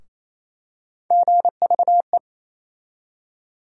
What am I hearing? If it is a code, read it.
Morse code, heard as a keyed tone, about 28 wpm, 700 Hz. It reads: GVE
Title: USA Call Sign Postfixes - 3 Letters 28wpm